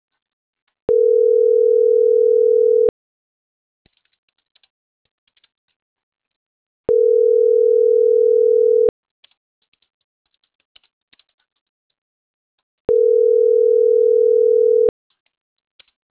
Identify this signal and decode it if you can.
Ringback tone (North America), 440 + 480 Hz.